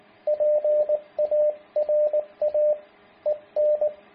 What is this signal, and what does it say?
Morse code, heard as a keyed tone, about 19 wpm, 600 Hz, with radio noise. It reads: PARA EN